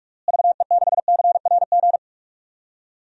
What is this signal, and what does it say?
Morse code, heard as a keyed tone, about 41 wpm, 700 Hz. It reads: VE6CRG
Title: DXCC-100 Call Signs - 6 Characters - Repeated 3x - 45wpm